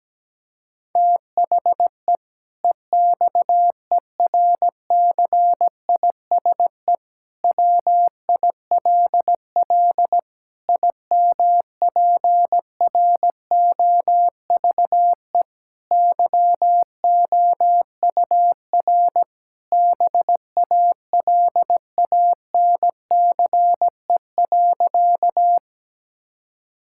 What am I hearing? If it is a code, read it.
Morse code, heard as a keyed tone, about 17 wpm, 700 Hz. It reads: THE EXERCISE WILL IMPROVE YOUR BALANCE.
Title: Sentences from Top 1000 Words - Review Part 6 - 17wpm